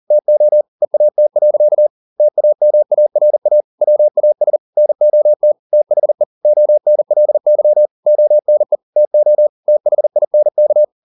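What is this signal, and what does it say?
Morse code, heard as a keyed tone, about 40 wpm, 600 Hz. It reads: TO EAT. TAMARA WAS NOT THE ONLY ONE TO THINK
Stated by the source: Morse code at 40 wpm, practice text, CW med Gnister.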